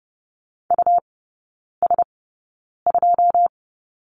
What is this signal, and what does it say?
Morse code, heard as a keyed tone, about 30 wpm, spaced as for 9 wpm, 700 Hz. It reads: US2
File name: Lesson 18e - Letter m - Three Characters 30wpm